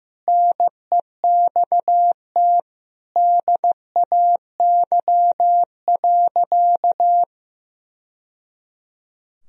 Morse code, heard as a keyed tone, about 15 wpm, 700 Hz. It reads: NEXT DAY.
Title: Sentences from Top 1000 Words - Review Part 1 - 15wpm